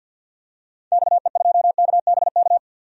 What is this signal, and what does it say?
Morse code, heard as a keyed tone, about 52 wpm, 700 Hz. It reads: XE2KBK
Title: DXCC-100 Call Signs - 6 Characters - 50wpm